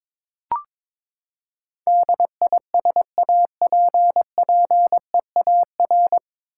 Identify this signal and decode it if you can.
Morse code, heard as a keyed tone, about 22 wpm, 700 Hz. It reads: DISAPPEAR
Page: Morse Code Ninja practice 22wpm